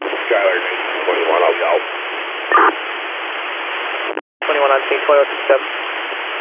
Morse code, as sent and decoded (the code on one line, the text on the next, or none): none